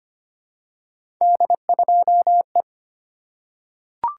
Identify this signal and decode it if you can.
Morse code, heard as a keyed tone, about 25 wpm, 700 Hz. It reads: D2E
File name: Three Letter-Number Combinations - 25wpm